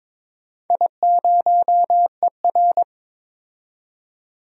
Morse code, heard as a keyed tone, about 22 wpm, 700 Hz. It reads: I0ER
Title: DXCC-100 Call Signs - 4 Characters - 22wpm